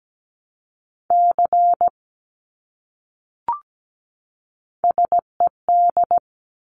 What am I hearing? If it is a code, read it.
Morse code, heard as a keyed tone, about 17 wpm, 700 Hz. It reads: C SED
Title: Abbreviations Repeated 3x - 17wpm